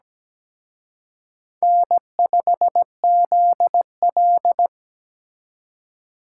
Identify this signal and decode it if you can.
Morse code, heard as a keyed tone, about 20 wpm, 700 Hz. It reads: N5ZL
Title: Speed Racing - USA 1x2 Call Signs 17wpm